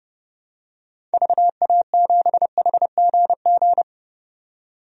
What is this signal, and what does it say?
Morse code, heard as a keyed tone, about 32 wpm, 700 Hz. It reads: VA7HGG
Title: DXCC-100 Call Signs - 6 Characters - 30wpm